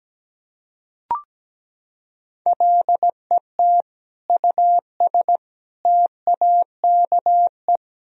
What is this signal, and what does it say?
Morse code, heard as a keyed tone, about 17 wpm, 700 Hz. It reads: LET US TAKE
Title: Sentences from Top 1000 Words - Review Part 2 - 17wpm